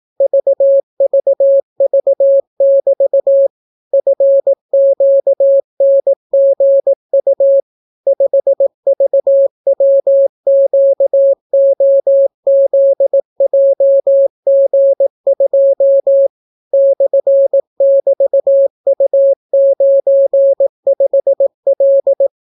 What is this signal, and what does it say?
Morse code, heard as a keyed tone, about 18 wpm, 550 Hz. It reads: VVV<BT> FQNGU 5VWQOZJG2 /<BT>U95L